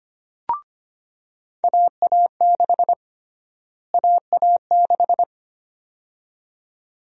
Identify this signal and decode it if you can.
Morse code, heard as a keyed tone, about 25 wpm, 700 Hz. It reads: AA6 AA6
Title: USA Call Sign Prefixes - Repeated 2x - 25wpm